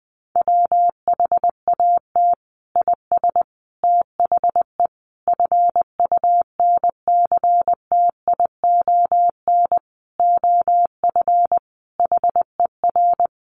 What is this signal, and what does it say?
Morse code, heard as a keyed tone, about 20 wpm, 700 Hz. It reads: WHAT IS THE FUNCTION OF HER
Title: Sentences from Top 1100 Words 20wpm